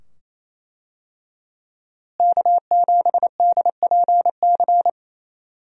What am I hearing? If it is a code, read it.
Morse code, heard as a keyed tone, about 27 wpm, 700 Hz. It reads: K7DPC
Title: USA Call Signs - 5 Characters - Repeated 2x - 28wpm